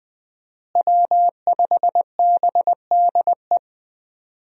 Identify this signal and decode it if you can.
Morse code, heard as a keyed tone, about 20 wpm, 700 Hz. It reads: W5BDE
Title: USA Call Signs - 5 Characters - Repeated 2x - 20wpm